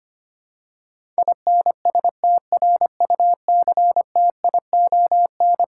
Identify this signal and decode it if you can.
Morse code, heard as a keyed tone, about 25 wpm, 700 Hz. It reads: INSTRUCTION
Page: morse-code-ninja-practice-25wpm